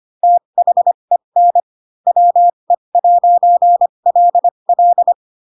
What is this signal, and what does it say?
Morse code, heard as a keyed tone, about 25 wpm, spaced as for 22 wpm, 700 Hz. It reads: THEN WE'LL
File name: Pinocchio / Pinocchio ch3 18 25 ik1hge